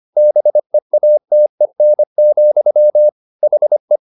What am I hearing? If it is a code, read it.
Morse code, heard as a keyed tone, about 25 wpm, 600 Hz. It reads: BEATEN, HE